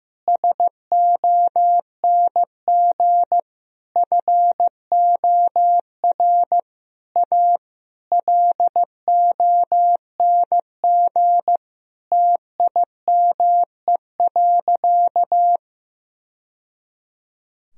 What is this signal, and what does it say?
Morse code, heard as a keyed tone, about 15 wpm, 700 Hz. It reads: SONG FOR A LONG TIME.
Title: Sentences from Top 1000 Words - Review Part 2 - 15wpm